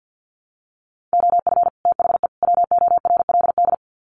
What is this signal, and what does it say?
Morse code, heard as a keyed tone, about 50 wpm, 700 Hz. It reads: OF THE WORLD
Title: Sets of 3 Words 50wpm